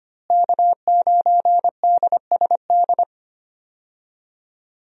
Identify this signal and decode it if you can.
Morse code, heard as a keyed tone, about 25 wpm, 700 Hz. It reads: K9DSD
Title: USA Call Signs - 5 Characters - Repeated 3x - 25wpm